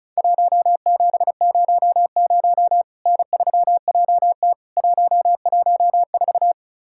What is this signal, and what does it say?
Morse code, heard as a keyed tone, about 36 wpm, 700 Hz. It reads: 1700 N3JT 114
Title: Worked All Europe DX Contest - QTC Exchange - 35wpm